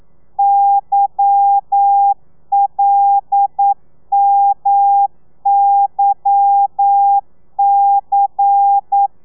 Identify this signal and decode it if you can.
Morse code, heard as a keyed tone, about 9 wpm, 800 Hz. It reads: YLMYC